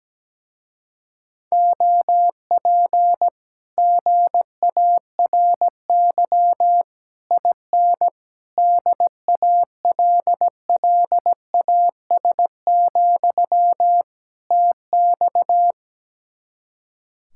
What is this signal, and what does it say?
Morse code, heard as a keyed tone, about 17 wpm, 700 Hz. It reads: OP GARY IN DALLAS, TX
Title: Speed-Racing Common QSO Phrases - 17wpm